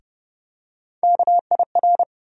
Morse code, heard as a keyed tone, about 30 wpm, 700 Hz. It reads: KIR
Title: USA Call Sign Postfixes - 3 Letters 30wpm